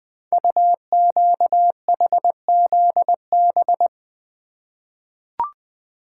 Morse code, heard as a keyed tone, about 20 wpm, 700 Hz. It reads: UQHZB
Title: Code Groups - 5 Letters - 20wpm